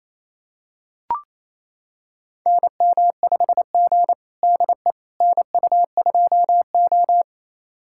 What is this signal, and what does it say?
Morse code, heard as a keyed tone, about 29 wpm, 700 Hz. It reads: NM5G DE NU2O
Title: USA Callsign DE Callsign - 4 Characters - 28wpm